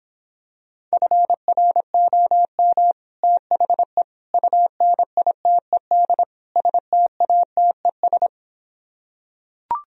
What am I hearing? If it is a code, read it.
Morse code, heard as a keyed tone, about 26 wpm, 700 Hz. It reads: FROM THE UNITED STATES